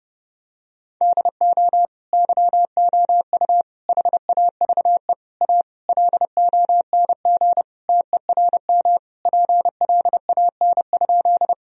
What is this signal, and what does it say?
Morse code, heard as a keyed tone, about 30 wpm, 700 Hz. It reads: DO YOU HAVE A LONG TERM PLAN?
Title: Sentences from Top 1000 Words - Review Part 4 - 30wpm